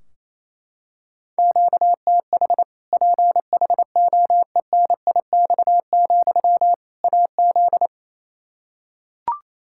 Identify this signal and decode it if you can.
Morse code, heard as a keyed tone, about 28 wpm, 700 Hz. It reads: QTH PHOENIX, AZ
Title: Practice QSO - Exchange 1 - Beginning - 28wpm